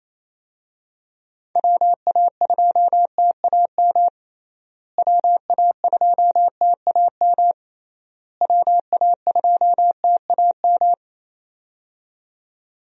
Morse code, heard as a keyed tone, about 28 wpm, 700 Hz. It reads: WA2TAM WA2TAM WA2TAM
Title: USA Call Signs - 6 Characters - Repeated 3x - 28wpm